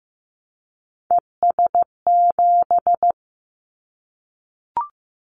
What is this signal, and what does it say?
Morse code, heard as a keyed tone, about 15 wpm, 700 Hz. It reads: ES7
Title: Callsign Prefixes for Top 100 DXCC - 15wpm